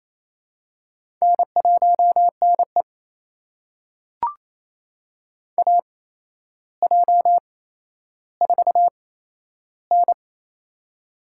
Morse code, heard as a keyed tone, about 27 wpm, spaced as for 12 wpm, 700 Hz. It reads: N1NE AJ4N
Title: USA Call Signs - 4 Characters - 8x Character Spacing - 28wpm